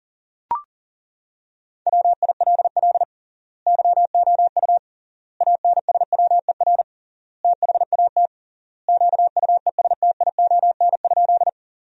Morse code, heard as a keyed tone, about 40 wpm, 700 Hz. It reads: WILL YOU ANSWER THAT QUESTION?
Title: 3x Word-Spacing - Sentences from Top 400 Words - 40wpm